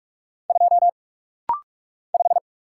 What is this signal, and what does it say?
Morse code, heard as a keyed tone, about 44 wpm, 700 Hz. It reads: 2 5
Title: Single Letter-Number Alt - Rapid-Fire 45wpm